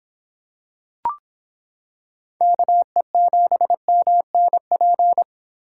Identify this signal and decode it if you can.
Morse code, heard as a keyed tone, about 26 wpm, 700 Hz. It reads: KE7MNP